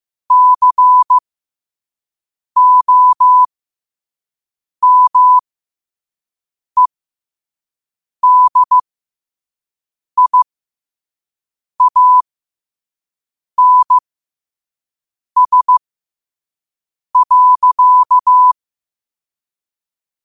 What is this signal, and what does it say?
Morse code, heard as a keyed tone, about 15 wpm, spaced as for 5 wpm, 1000 Hz. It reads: COMEDIANS.
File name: 5wpm